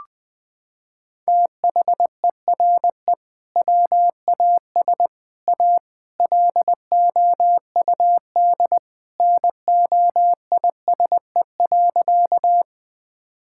Morse code, heard as a keyed tone, about 21 wpm, 700 Hz. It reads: THERE WAS A LOUD NOISE.
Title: Sentences with 3-Word Phrases - 20wpm